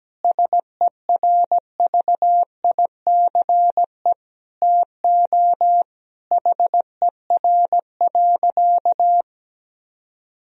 Morse code, heard as a keyed tone, about 17 wpm, 700 Hz. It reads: SERVICE TO HER.
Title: Sentences from Top 1000 Words - Review Part 4 - 17wpm